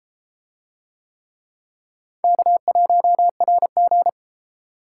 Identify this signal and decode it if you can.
Morse code, heard as a keyed tone, about 32 wpm, 700 Hz. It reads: K1RG